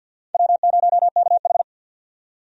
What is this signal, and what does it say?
Morse code, heard as a keyed tone, about 52 wpm, 700 Hz. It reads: W0KH
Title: USA Call Signs - 4 Characters - 50wpm